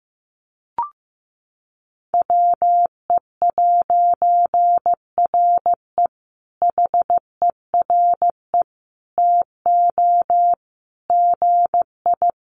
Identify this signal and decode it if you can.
Morse code, heard as a keyed tone, about 15 wpm, 700 Hz. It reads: WE'RE HERE TO GI
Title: A Sound Of Thunder 15wpm